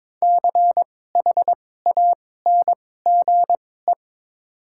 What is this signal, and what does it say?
Morse code, heard as a keyed tone, about 21 wpm, spaced as for 16 wpm, 700 Hz. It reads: CHANGE
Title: 2x Character Spacing - Top 200 Words - 22wpm